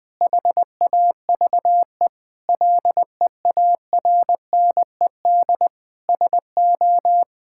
Morse code, heard as a keyed tone, about 20 wpm, 700 Hz. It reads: HAVE LEARNED SO
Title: Sentences with 5-Word Phrases - Spoken Beginning - 20wpm